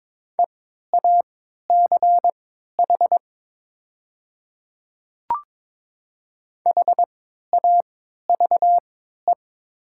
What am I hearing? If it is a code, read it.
Morse code, heard as a keyed tone, about 22 wpm, spaced as for 12 wpm, 700 Hz. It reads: EACH HAVE
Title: 3x Character Spacing - Top 100 Words - 22wpm